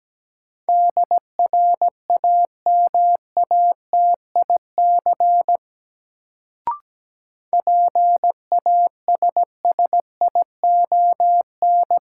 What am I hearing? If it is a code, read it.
Morse code, heard as a keyed tone, about 17 wpm, 700 Hz. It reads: DRAMATIC PASSION
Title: Sentences from Top 1300 Words - New Word Drill - 17wpm